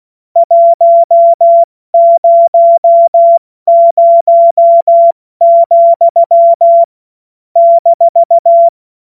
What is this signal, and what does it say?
Morse code, heard as a keyed tone, about 16 wpm, 680 Hz. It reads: 100, -
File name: Witz-21-Oktober-2025-70-F80